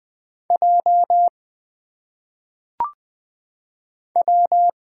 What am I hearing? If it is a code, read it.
Morse code, heard as a keyed tone, about 20 wpm, 700 Hz. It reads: J W
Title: Single Letters - Phonetic Pronunciation 20wpm